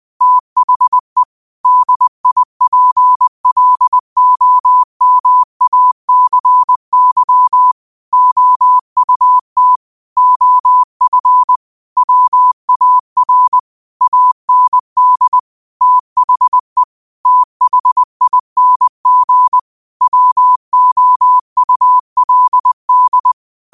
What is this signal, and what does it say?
Morse code, heard as a keyed tone, about 20 wpm, 1000 Hz. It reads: THE DIPLOMACY OUT OF WAR AND THE THING WOULD
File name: qotdfiles / 20wpm